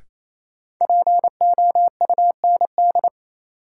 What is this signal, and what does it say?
Morse code, heard as a keyed tone, about 28 wpm, 700 Hz. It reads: POUND
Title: Sentences from Top 1000 Words - Review Part 2 - New Word Drill - 28wpm